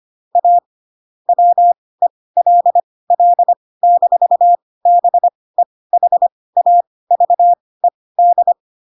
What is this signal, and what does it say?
Morse code, heard as a keyed tone, about 25 wpm, spaced as for 18 wpm, 700 Hz. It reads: A WELL-BEHAVED
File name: Pinocchio ch3 12 25 ik1hge